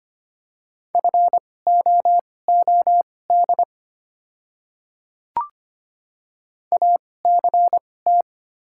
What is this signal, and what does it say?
Morse code, heard as a keyed tone, about 25 wpm, spaced as for 18 wpm, 700 Hz. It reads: FOOD ACT
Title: 2x Character Spacing - Top 300 Words - 25wpm